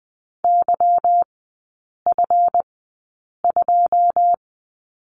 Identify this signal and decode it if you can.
Morse code, heard as a keyed tone, about 20 wpm, spaced as for 8 wpm, 700 Hz. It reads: YF2
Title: 172 Lesson 28e - Number 9 - Three Characters 20wpm